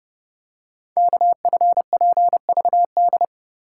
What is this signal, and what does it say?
Morse code, heard as a keyed tone, about 30 wpm, 700 Hz. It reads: KFPVD
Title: Code Groups - 5 Letters - 30wpm